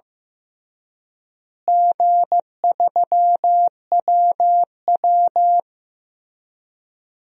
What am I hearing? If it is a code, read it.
Morse code, heard as a keyed tone, about 15 wpm, 700 Hz. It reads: G3WW
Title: DXCC-100 Call Signs - 4 Characters - Repeated 3x - 15wpm